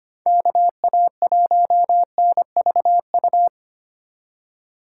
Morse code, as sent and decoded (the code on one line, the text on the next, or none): -.- .- .---- -. ...- ..-
KA1NVU